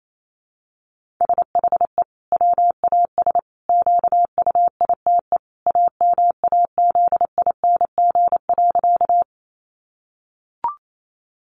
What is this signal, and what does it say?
Morse code, heard as a keyed tone, about 28 wpm, 700 Hz. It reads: SHE WAS QUITE AMAZING.
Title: Sentences from Top 1500 Words 28wpm